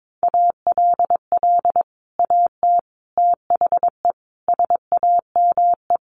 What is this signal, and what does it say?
Morse code, heard as a keyed tone, about 22 wpm, 700 Hz. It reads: ALL AT THE SAME
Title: Sets of 5 Words Encore2 22wpm